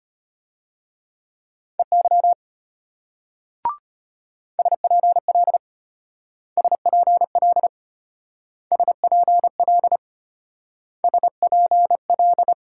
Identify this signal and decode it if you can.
Morse code, heard as a keyed tone, about 32 wpm, 700 Hz. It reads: EY SPL SPL SPL SPL